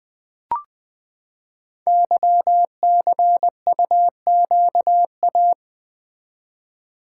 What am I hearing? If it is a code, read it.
Morse code, heard as a keyed tone, about 20 wpm, 700 Hz. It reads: YCUQA
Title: Code Groups - 5 Letters - 20wpm